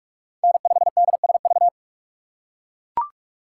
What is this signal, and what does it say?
Morse code, heard as a keyed tone, about 43 wpm, 700 Hz. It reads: N5DSV
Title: USA Call Signs - 5 Characters - 2x Character Spacing - 45wpm